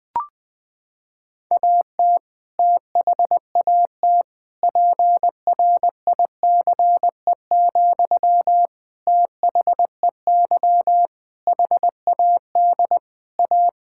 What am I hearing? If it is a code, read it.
Morse code, heard as a keyed tone, about 20 wpm, 700 Hz. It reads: AT THAT PRICE, THEY HAD A
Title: Sentences from Top 1000 Words - Review Part 5 - 20wpm